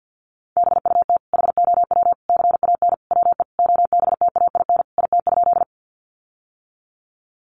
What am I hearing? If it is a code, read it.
Morse code, heard as a keyed tone, about 50 wpm, 700 Hz. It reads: BUT HOW CAN WE OBTAIN IT?
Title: Sentences from Top 1300 Words 50wpm